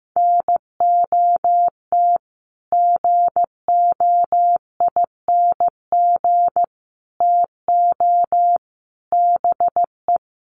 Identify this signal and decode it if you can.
Morse code, heard as a keyed tone, about 15 wpm, 700 Hz. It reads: NOT GOING TO BE
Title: Sets of 5 Words 15wpm